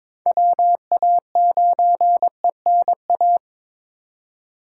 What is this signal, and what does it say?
Morse code, heard as a keyed tone, about 22 wpm, 700 Hz. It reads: WA9ENA
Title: K1USN SST QSO Practice - 22wpm